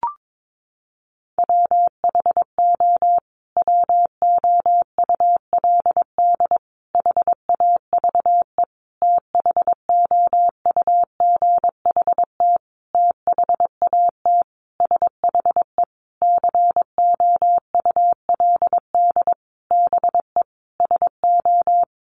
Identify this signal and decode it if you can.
Morse code, heard as a keyed tone, about 22 wpm, 700 Hz. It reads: WHO WOULD HAVE THOUGHT THAT SHE COULD BE SO
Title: Sentences from Top 1200 Words 22wpm